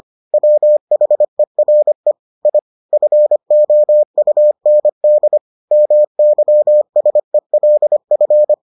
Morse code, heard as a keyed tone, about 25 wpm, 600 Hz. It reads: WHERE I FOUND MYSELF